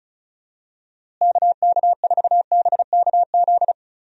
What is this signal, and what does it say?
Morse code, heard as a keyed tone, about 34 wpm, 700 Hz. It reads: KK4BKZ
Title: USA Call Signs - 6 Characters - Repeated 3x - 35wpm